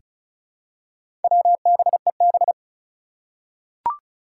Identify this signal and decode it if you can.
Morse code, heard as a keyed tone, about 34 wpm, 700 Hz. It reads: WBEB
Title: Four Challenging Letter-Number Combinations 35wpm